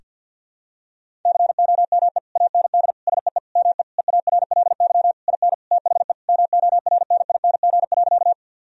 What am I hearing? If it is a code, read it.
Morse code, heard as a keyed tone, about 50 wpm, 700 Hz. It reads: COME AND SEE ME EARLY IN THE MORNING.
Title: Sentences from Top 1000 Words - Review Part 2 - 50wpm